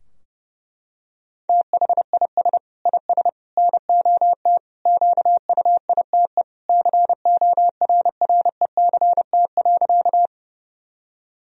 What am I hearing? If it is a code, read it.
Morse code, heard as a keyed tone, about 32 wpm, 700 Hz. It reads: THIS IS NOT QUITE CORRECT.
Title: Sentences from Top 1000 Words - Review Part 2 - 30wpm